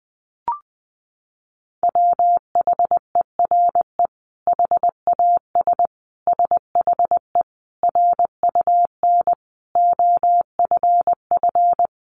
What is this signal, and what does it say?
Morse code, heard as a keyed tone, about 20 wpm, 700 Hz. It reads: WHERE HAS SHE RUN OFF
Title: Sentences from Top 300 Words 20wpm